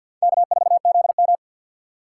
Morse code, heard as a keyed tone, about 51 wpm, 700 Hz. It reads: K4ZM
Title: USA Call Signs - 3x Character Spacing - 50wpm